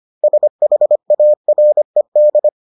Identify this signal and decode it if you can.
Morse code, heard as a keyed tone, about 25 wpm, 600 Hz. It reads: SHARED